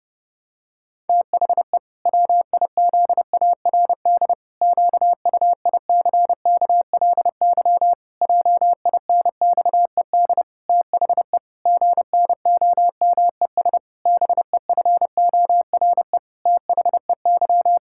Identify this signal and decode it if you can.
Morse code, heard as a keyed tone, about 30 wpm, 700 Hz. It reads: THE WIZARD QUICKLY JINXED THE GNOMES BEFORE THEY
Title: Pangram Challenge v2 - 30wpm